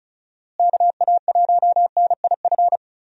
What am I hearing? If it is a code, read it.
Morse code, heard as a keyed tone, about 36 wpm, 700 Hz. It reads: KA1NIF
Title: USA Call Signs - 6 Characters - Repeated 3x - 35wpm